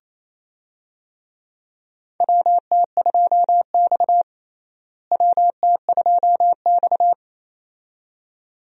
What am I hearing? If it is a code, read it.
Morse code, heard as a keyed tone, about 27 wpm, 700 Hz. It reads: WT2X WT2X
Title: USA Call Signs - 4 Characters - Repeated 2x - 28wpm